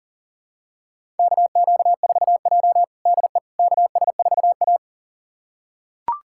Morse code, heard as a keyed tone, about 40 wpm, 700 Hz. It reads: KQ4J DE KS4A
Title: USA Callsign DE Callsign - 4 Characters - 40wpm